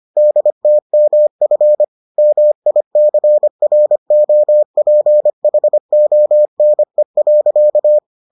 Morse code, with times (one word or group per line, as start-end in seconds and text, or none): 0.2-1.8 DTMF
2.2-8.0 MICROPHONE.